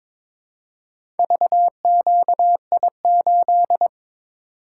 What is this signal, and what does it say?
Morse code, heard as a keyed tone, about 22 wpm, 700 Hz. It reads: VQI8
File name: practice / 22wpm / Four Letter-Number Combinations - 22wpm